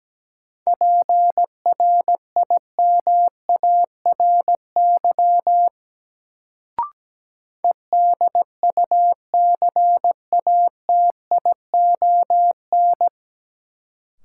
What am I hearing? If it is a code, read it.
Morse code, heard as a keyed tone, about 17 wpm, 700 Hz. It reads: PRIMARY EDUCATION
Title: Sentences from Top 1100 Words - New Word Drill Alt - 17wpm